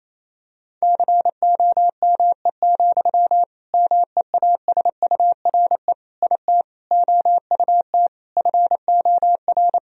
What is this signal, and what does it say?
Morse code, heard as a keyed tone, about 28 wpm, 700 Hz. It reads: COME, MEASURE IT OUT FOR
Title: Sentences from Top 1000 Words - Review Part 2 - 28wpm